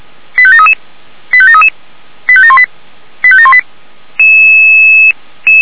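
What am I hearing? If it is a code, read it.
Five-tone selective call (ZVEI), heard as quick runs of five tones, tones of about 70 ms. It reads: 85733 85733 85718 85718
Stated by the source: ZVEI five-tone call received on SDR.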